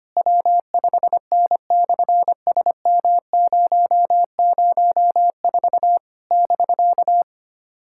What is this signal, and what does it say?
Morse code, heard as a keyed tone, about 25 wpm, 700 Hz. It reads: W5N/SM004 <BK>
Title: SOTA QSO Practice 25wpm